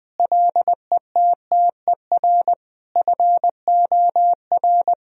Morse code, heard as a keyed tone, about 20 wpm, 700 Hz. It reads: LETTER FOR
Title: Sentences from Top 1000 Words - Review Part 2 - 20wpm